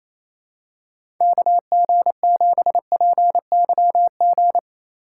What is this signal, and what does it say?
Morse code, heard as a keyed tone, about 28 wpm, 700 Hz. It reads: KG7PYG
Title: USA Call Signs - 3x Character Spacing - 28wpm